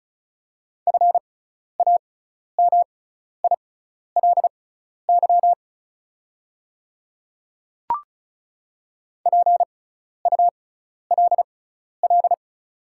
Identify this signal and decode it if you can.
Morse code, heard as a keyed tone, about 34 wpm, spaced as for 12 wpm, 700 Hz. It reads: FAMILY PULL
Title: 6x Character Spacing - Top 500 Words - 35wpm